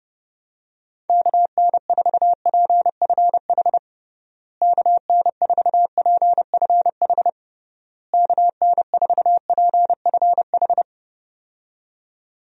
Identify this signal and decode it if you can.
Morse code, heard as a keyed tone, about 30 wpm, 700 Hz. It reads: KN4PFH KN4PFH KN4PFH
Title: USA Call Signs - 6 Characters - Repeated 3x - 30wpm